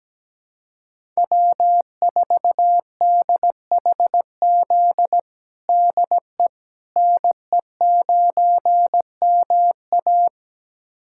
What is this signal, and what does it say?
Morse code, heard as a keyed tone, about 17 wpm, 700 Hz. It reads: W4DHZ DE NE9MA
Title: USA Callsign DE Callsign - 5 Characters - 17wpm